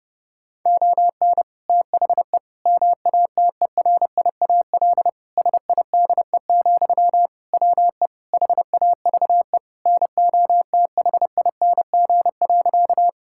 Morse code, heard as a keyed tone, about 32 wpm, 700 Hz. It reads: ON THE MATERIAL SIDE, WE HAVE NOTHING.
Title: Sentences from Top 1000 Words - Review Part 5 - 30wpm